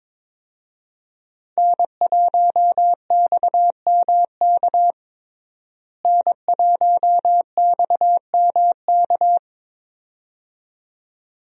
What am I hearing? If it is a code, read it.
Morse code, heard as a keyed tone, about 22 wpm, 700 Hz. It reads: N1XMK N1XMK
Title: USA Call Signs - 5 Characters - Repeated 2x - 22wpm